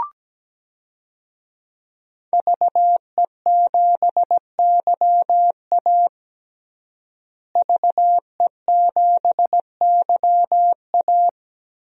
Morse code, heard as a keyed tone, about 17 wpm, 700 Hz. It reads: VE7YA VE7YA
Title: DXCC-100 Call Signs - 5 Characters - Repeated 2x - 17wpm